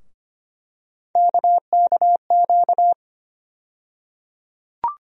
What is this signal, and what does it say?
Morse code, heard as a keyed tone, about 25 wpm, 700 Hz. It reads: KKQ
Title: USA Call Sign Postfixes - 3 Letters 25wpm